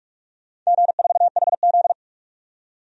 Morse code, heard as a keyed tone, about 46 wpm, 700 Hz. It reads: G4HZ
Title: DXCC-100 Call Signs - 4 Characters - 45wpm